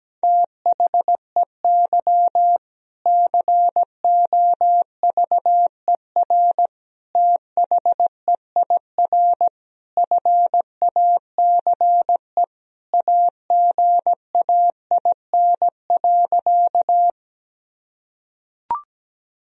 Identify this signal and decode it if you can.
Morse code, heard as a keyed tone, about 17 wpm, 700 Hz. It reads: THEY COVER THEIR FACE AGAIN.
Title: Sentences from Top 1000 Words - Review Part 1 - 17wpm